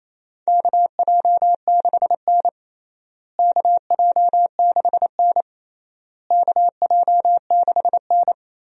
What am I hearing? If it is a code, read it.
Morse code, heard as a keyed tone, about 28 wpm, 700 Hz. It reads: KJ6N KJ6N KJ6N
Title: USA Call Signs - 4 Characters - Repeated 3x - 28wpm